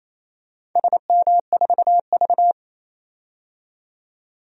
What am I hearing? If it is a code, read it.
Morse code, heard as a keyed tone, about 28 wpm, 700 Hz. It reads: SM4V
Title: DXCC-100 Call Signs - 4 Characters - 28wpm